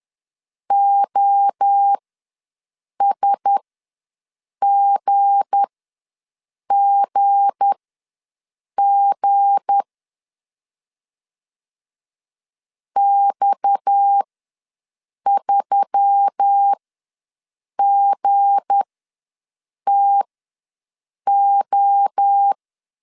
Morse code, heard as a keyed tone, about 10 wpm, spaced as for 6 wpm, 800 Hz. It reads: OSGGG X3GTO
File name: LEZIONI / 7 GF9Z / 7 G ws 10wpm 3843